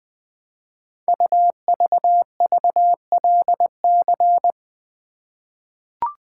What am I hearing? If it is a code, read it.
Morse code, heard as a keyed tone, about 20 wpm, 700 Hz. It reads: UVVLC
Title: Code Groups - 5 Letters - 20wpm